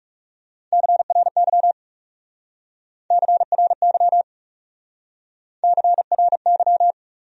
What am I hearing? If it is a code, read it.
Morse code, heard as a keyed tone, about 40 wpm, 700 Hz. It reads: CRY CRY CRY